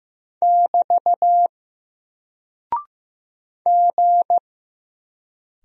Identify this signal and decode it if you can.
Morse code, heard as a keyed tone, about 15 wpm, 700 Hz. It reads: <BT> G
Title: Single Character (Letter:Number:Punctuation:Prosign) 15wpm